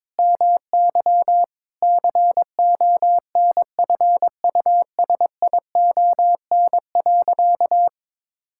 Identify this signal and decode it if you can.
Morse code, heard as a keyed tone, about 22 wpm, 700 Hz. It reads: MY CONFUSION.
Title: Sentences from Top 1600 Words 22wpm